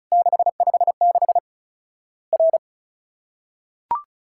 Morse code, heard as a keyed tone, about 36 wpm, 700 Hz. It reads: 656 R
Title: Worked All Europe DX Contest - QTC Exchange - 35wpm